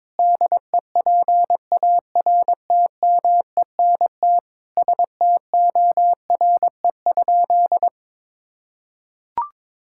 Morse code, heard as a keyed tone, about 22 wpm, 700 Hz. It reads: DEPARTMENT STORE?
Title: Sentences from Top 1000 Words - Review Part 5 - 22wpm